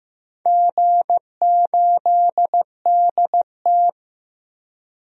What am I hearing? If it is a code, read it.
Morse code, heard as a keyed tone, about 15 wpm, 700 Hz. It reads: G8DT
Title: DXCC-100 Call Signs - 4 Characters - 15wpm